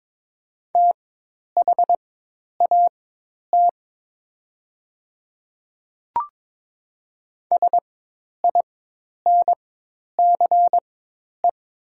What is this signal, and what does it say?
Morse code, heard as a keyed tone, about 22 wpm, spaced as for 10 wpm, 700 Hz. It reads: THAT SINCE
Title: 4x Character Spacing - Top 300 Words - 22wpm